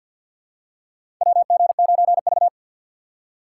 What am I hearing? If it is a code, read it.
Morse code, heard as a keyed tone, about 48 wpm, 700 Hz. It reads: WG9V
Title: USA Call Signs - 3x Character Spacing - 50wpm